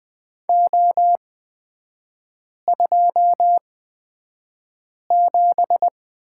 Morse code, heard as a keyed tone, about 20 wpm, spaced as for 5 wpm, 700 Hz. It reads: O27
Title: 236 Lesson 38e - Letter q - Three Characters 20wpm